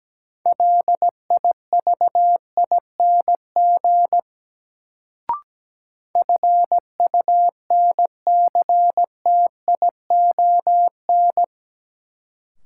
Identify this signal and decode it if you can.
Morse code, heard as a keyed tone, about 17 wpm, 700 Hz. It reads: LIVING FUNCTION
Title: Sentences from Top 1100 Words - New Word Drill Alt - 17wpm